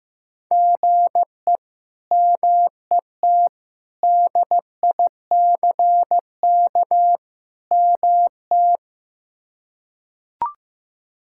Morse code, heard as a keyed tone, about 15 wpm, 700 Hz. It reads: GE MET DICK MT
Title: K1USN SST QSO Practice - 15wpm